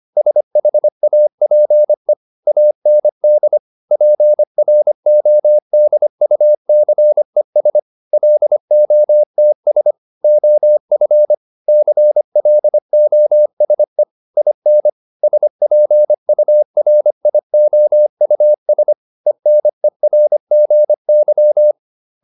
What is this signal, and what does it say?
Morse code, heard as a keyed tone, about 25 wpm, 600 Hz. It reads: SHAPE AND PRODUCES LOTS OF CLOSE IN SPURIOUS ENERGY